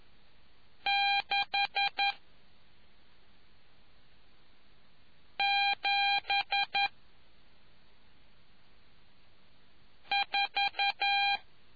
Morse code, heard as a keyed tone, about 11 wpm, 800 Hz. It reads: 6 7 4